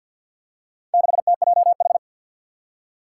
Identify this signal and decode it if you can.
Morse code, heard as a keyed tone, about 52 wpm, 700 Hz. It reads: 6TJH
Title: Four Challenging Letter-Number Combinations 50wpm